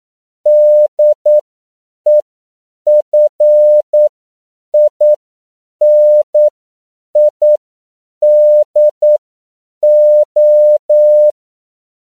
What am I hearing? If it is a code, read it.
Morse code, heard as a keyed tone, about 9 wpm, spaced as for 7 wpm, 600 Hz. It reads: DEFINIDO
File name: CW / 10PPM